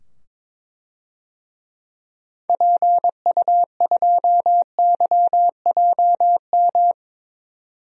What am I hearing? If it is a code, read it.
Morse code, heard as a keyed tone, about 22 wpm, 700 Hz. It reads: PU2YJM
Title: DXCC-100 Call Signs - 6 Characters - 22wpm